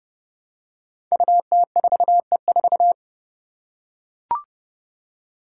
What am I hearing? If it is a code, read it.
Morse code, heard as a keyed tone, about 30 wpm, 700 Hz. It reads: UT4E4
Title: Five Challenging Letter-Number Combinations 30wpm